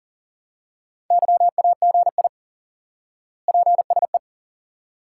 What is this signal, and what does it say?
Morse code, heard as a keyed tone, about 40 wpm, 700 Hz. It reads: YAGI PSE
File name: IQE Course 3 - Lesson 89 - YAGI - Two Elements - 40wpm